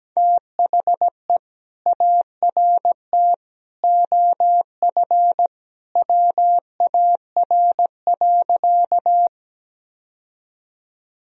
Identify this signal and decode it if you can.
Morse code, heard as a keyed tone, about 17 wpm, 700 Hz. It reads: THE ART OF WAR.
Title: Sentences from Top 1000 Words - Review Part 3 - 17wpm